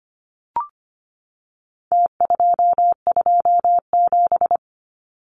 Morse code, heard as a keyed tone, about 25 wpm, 700 Hz. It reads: T227
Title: Numbers - Contest Style - 4-Digit 25wpm